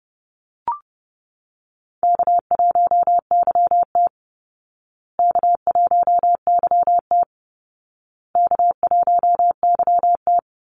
Morse code, heard as a keyed tone, about 30 wpm, 700 Hz. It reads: K1YT K1YT K1YT
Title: Easy Call Signs Repeated 3x - 30wpm